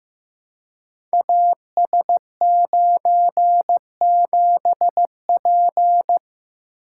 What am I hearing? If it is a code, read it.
Morse code, heard as a keyed tone, about 15 wpm, 700 Hz. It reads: AS97P
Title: Code Groups - 5 Letter:Numbers - 15wpm